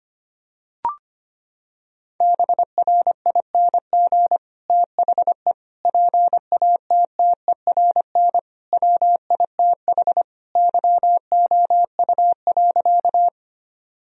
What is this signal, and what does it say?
Morse code, heard as a keyed tone, about 25 wpm, 700 Hz. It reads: BRING THE PATTERN WITH YOU.
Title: Sentences from Top 1000 Words - Review Part 2 - 25wpm